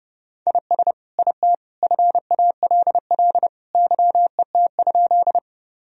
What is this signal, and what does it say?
Morse code, heard as a keyed tone, about 30 wpm, 700 Hz. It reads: IS IT FALL YET?
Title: Sentences from Top 1000 Words - Review Part 2 - 30wpm